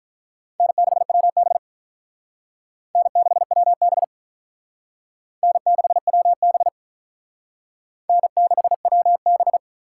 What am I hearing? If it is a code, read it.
Morse code, heard as a keyed tone, about 45 wpm, 700 Hz. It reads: N6WB N6WB N6WB N6WB